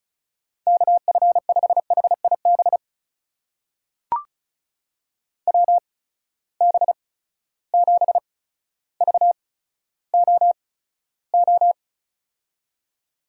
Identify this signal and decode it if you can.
Morse code, heard as a keyed tone, about 34 wpm, spaced as for 15 wpm, 700 Hz. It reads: KF5HIB WB7VOO
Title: USA Call Signs - 6 Characters - 8x Character Spacing - 35wpm